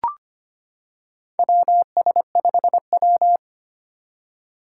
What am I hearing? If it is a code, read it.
Morse code, heard as a keyed tone, about 25 wpm, 700 Hz. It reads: WS5W
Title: Four Challenging Letter-Number Combinations 25wpm